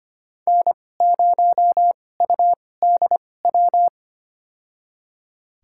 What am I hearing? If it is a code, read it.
Morse code, heard as a keyed tone, about 25 wpm, spaced as for 18 wpm, 700 Hz. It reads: N0UDW
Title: USA Call Signs - 5 Characters - 2x Character Spacing - 25wpm